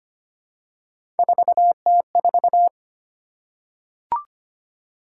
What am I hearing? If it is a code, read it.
Morse code, heard as a keyed tone, about 25 wpm, 700 Hz. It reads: 4T4
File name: Three Letter-Number Combinations - 25wpm